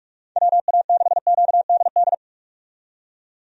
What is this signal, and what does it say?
Morse code, heard as a keyed tone, about 44 wpm, 700 Hz. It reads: WA6QDD
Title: USA Call Signs - 6 Characters - 45wpm